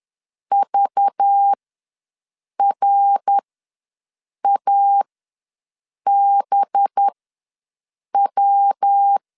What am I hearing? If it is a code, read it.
Morse code, heard as a keyed tone, about 10 wpm, spaced as for 6 wpm, 800 Hz. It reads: VRABW
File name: LEZIONI / 5 LBVC / 5 LBV ws 10wpm 3645